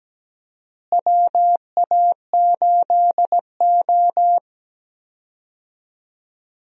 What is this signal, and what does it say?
Morse code, heard as a keyed tone, about 17 wpm, 700 Hz. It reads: WA8O
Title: USA Call Signs - 4 Characters - Repeated 3x - 17wpm